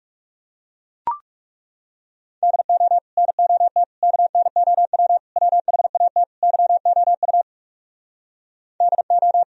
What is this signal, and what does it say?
Morse code, heard as a keyed tone, about 37 wpm, 700 Hz. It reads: DO NOT KNOW WHAT YOU DO
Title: Speed-Racing - Sets of 5 Words 30wpm